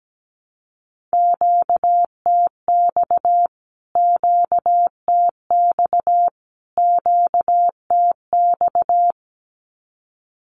Morse code, heard as a keyed tone, about 17 wpm, 700 Hz. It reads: QTX QTX QTX
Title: Q-Codes Repeated 3x 17wpm